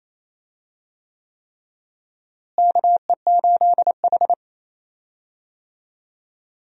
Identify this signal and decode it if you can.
Morse code, heard as a keyed tone, about 28 wpm, 700 Hz. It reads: KE8H
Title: USA Call Signs - 4 Characters - Repeated 3x - 28wpm